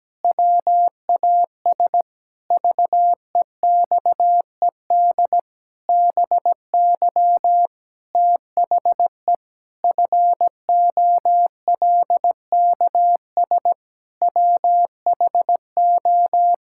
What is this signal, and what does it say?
Morse code, heard as a keyed tone, about 17 wpm, 700 Hz. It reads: WAS VEXED BY THE FOLKS WHO
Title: Pangram Challenge v2 - 17wpm